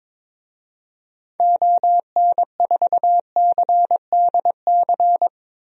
Morse code, heard as a keyed tone, about 22 wpm, 700 Hz. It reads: ON4CDC
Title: DXCC-100 Call Signs - 6 Characters - Repeated 3x - 22wpm